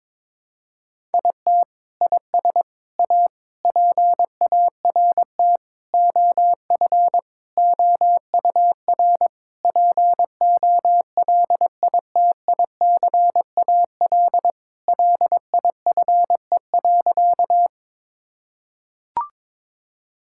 Morse code, heard as a keyed tone, about 22 wpm, 700 Hz. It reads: IT IS A PART OF OUR POLITICAL LIFE.
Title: Sentences from Top 1000 Words - Review Part 5 - 22wpm